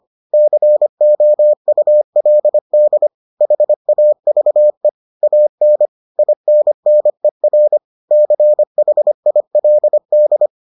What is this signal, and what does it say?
Morse code, heard as a keyed tone, about 25 wpm, 600 Hz. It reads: COULD HAVE AN INNER CHILD